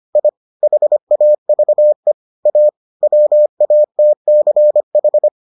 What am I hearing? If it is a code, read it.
Morse code, heard as a keyed tone, about 25 wpm, 600 Hz. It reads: I HAVE A WATCH